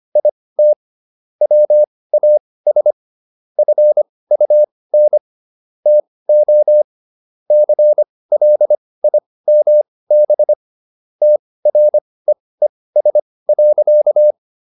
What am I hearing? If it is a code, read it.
Morse code, heard as a keyed tone, about 25 wpm, spaced as for 18 wpm, 600 Hz. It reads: IT WAS FUN TO CLIMB TREES.